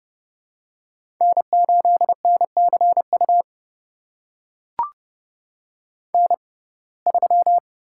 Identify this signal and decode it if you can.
Morse code, heard as a keyed tone, about 30 wpm, spaced as for 22 wpm, 700 Hz. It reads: N8NCU N3
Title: USA Call Signs - 5 Characters - 6x Character Spacing - 30wpm